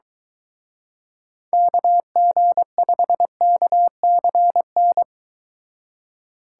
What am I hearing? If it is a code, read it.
Morse code, heard as a keyed tone, about 23 wpm, 700 Hz. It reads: KG5KCN